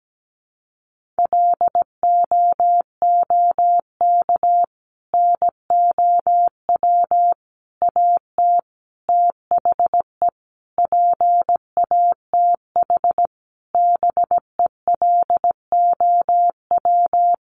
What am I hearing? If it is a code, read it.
Morse code, heard as a keyed tone, about 17 wpm, 700 Hz. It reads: LOOK NOW AT THE PATH BELOW
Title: Sentences from Top 1400 Words 17wpm